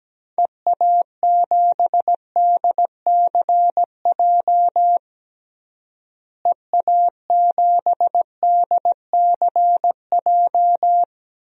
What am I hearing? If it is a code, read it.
Morse code, heard as a keyed tone, about 17 wpm, 700 Hz. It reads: EA7DCJ EA7DCJ
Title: DXCC-100 Call Signs - 6 Characters - Repeated 2x - 17wpm